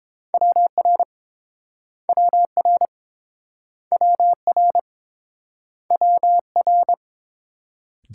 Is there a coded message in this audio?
Yes: Morse code from 0.3 to 6.9 s, heard as a keyed tone, about 29 wpm, 700 Hz. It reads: WR WR WR WR